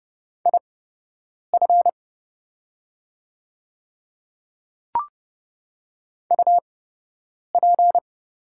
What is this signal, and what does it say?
Morse code, heard as a keyed tone, about 31 wpm, spaced as for 8 wpm, 700 Hz. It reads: IF UP
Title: 8x Character Spacing - Top 100 Words - 30wpm